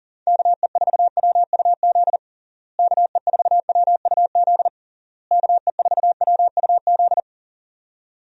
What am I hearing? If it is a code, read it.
Morse code, heard as a keyed tone, about 40 wpm, 700 Hz. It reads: KE4WUZ KE4WUZ KE4WUZ
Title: USA Call Signs - 6 Characters - Repeated 3x - 40wpm